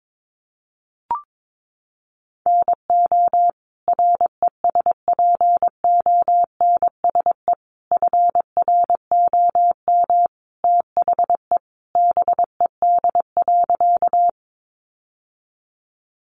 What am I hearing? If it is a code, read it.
Morse code, heard as a keyed tone, about 22 wpm, 700 Hz. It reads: NO RESPONSE FROM THE BED.
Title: Sentences from Top 1400 Words 22wpm